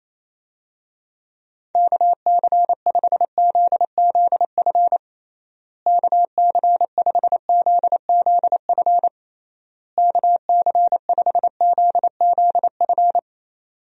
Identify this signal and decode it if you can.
Morse code, heard as a keyed tone, about 28 wpm, 700 Hz. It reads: KC5ZZF KC5ZZF KC5ZZF
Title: USA Call Signs - 6 Characters - Repeated 3x - 28wpm